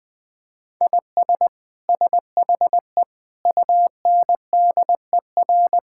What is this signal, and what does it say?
Morse code, heard as a keyed tone, about 21 wpm, 700 Hz. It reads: IS SHE UNDER
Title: Sentences from Top 1600 Words 20wpm